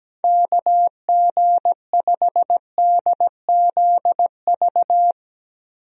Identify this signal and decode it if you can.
Morse code, heard as a keyed tone, about 17 wpm, 700 Hz. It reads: KG5DZV
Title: USA Call Signs - 6 Characters - 17wpm